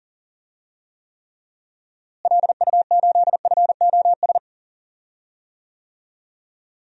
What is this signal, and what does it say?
Morse code, heard as a keyed tone, about 39 wpm, 700 Hz. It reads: LU8FOS
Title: DXCC-100 Call Signs - 6 Characters - Repeated 3x - 40wpm